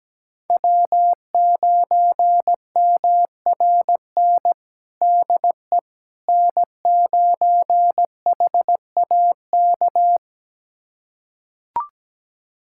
Morse code, heard as a keyed tone, about 17 wpm, 700 Hz. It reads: W9MRN DE N9HAK
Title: USA Callsign DE Callsign - 5 Characters - 17wpm